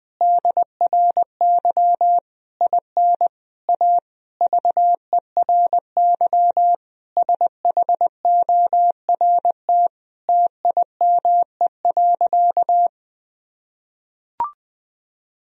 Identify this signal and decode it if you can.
Morse code, heard as a keyed tone, about 20 wpm, 700 Hz. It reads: DRY IN A VERY SHORT TIME.
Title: Sentences from Top 1000 Words - Review Part 3 - 20wpm